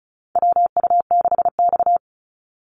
Morse code, heard as a keyed tone, about 34 wpm, 700 Hz. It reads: WU6X
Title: Call Signs - 35wpm